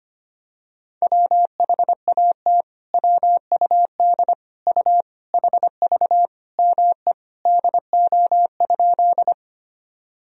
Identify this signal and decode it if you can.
Morse code, heard as a keyed tone, about 25 wpm, 700 Hz. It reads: WHAT WUD U HV ME DO?
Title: Sentences from Top 200 Words and Abbreviations 25wpm